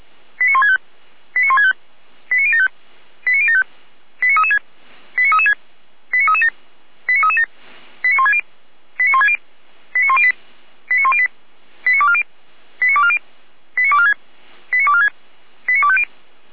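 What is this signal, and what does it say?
Five-tone selective call (ZVEI), heard as quick runs of five tones, tones of about 80 ms. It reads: 89165 89165 89075 89075 89336 89336 89337 89337 89160 89160 89108 89108 89240 89240 89246 89246 89250
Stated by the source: ZVEI five-tone selective call 89167, received by SDR, feuerwehr-orlamuende.org fire brigade.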